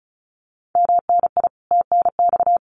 Morse code, heard as a keyed tone, about 34 wpm, 700 Hz. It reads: MNI TNX
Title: Common QSO Phrases Encore 35wpm